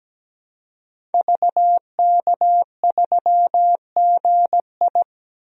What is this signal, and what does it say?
Morse code, heard as a keyed tone, about 17 wpm, 700 Hz. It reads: VK3GI
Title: DXCC-100 Call Signs - 5 Characters - 17wpm